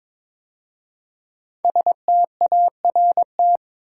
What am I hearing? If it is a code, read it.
Morse code, heard as a keyed tone, about 22 wpm, 700 Hz. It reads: START
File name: 5 Letter Words - First 3 Letters Given - 22wpm